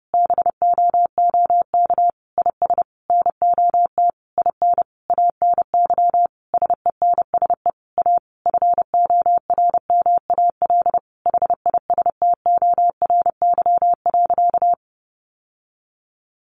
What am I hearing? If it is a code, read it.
Morse code, heard as a keyed tone, about 30 wpm, 700 Hz. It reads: BOOK IS NOT IN ANY SENSE A FORMAL HISTORY.
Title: Sentences from Top 1300 Words 30wpm